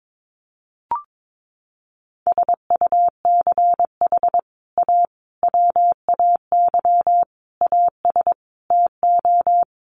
Morse code, heard as a keyed tone, about 22 wpm, 700 Hz. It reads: SUCH A WAY AS TO
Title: Sets of 5 Words Encore2 22wpm